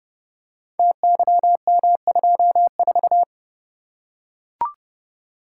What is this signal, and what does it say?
Morse code, heard as a keyed tone, about 30 wpm, 700 Hz. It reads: TYM24
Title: Code Groups - 5 Letter:Numbers - 30wpm